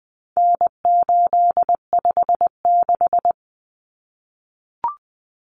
Morse code, heard as a keyed tone, about 20 wpm, 700 Hz. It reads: N856
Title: Common Cut Numbers - 4-Digit 20wpm